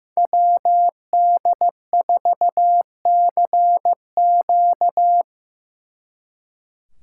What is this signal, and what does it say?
Morse code, heard as a keyed tone, about 18 wpm, 700 Hz. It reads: WD4CQ
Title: Speed Racing - USA 2x2 Call Signs 15wpm